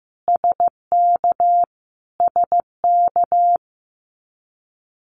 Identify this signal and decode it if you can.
Morse code, heard as a keyed tone, about 15 wpm, 700 Hz. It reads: SK SK
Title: Canadian Provinces and Territories - Repeated 3x - 15wpm